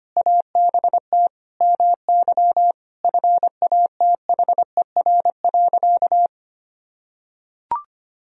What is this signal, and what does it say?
Morse code, heard as a keyed tone, about 25 wpm, 700 Hz. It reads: ABT MY FATHER.
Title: Sentences from Top 400 Words and Abbreviations 25wpm